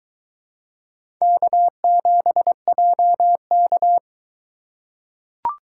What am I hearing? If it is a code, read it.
Morse code, heard as a keyed tone, about 22 wpm, 700 Hz. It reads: K7JK